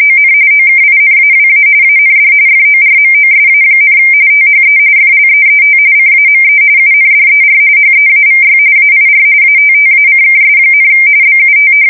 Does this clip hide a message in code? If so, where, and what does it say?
Yes: RTTY from 0.0 to 11.7 s, about 45.45 baud. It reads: ON OKINAWA AND FORMOSA, THE FLEET CAME UNDER HEAVY AIR ATTACKS, BUT